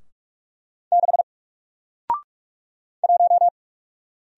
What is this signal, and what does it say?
Morse code, heard as a keyed tone, about 45 wpm, 700 Hz. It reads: 6 1
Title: Single Letter-Number 45wpm